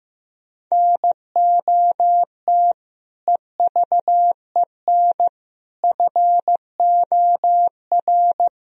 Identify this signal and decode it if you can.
Morse code, heard as a keyed tone, about 15 wpm, 700 Hz. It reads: NOT EVEN FOR
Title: Sentences from Top 1000 Words - Review Part 3 - 15wpm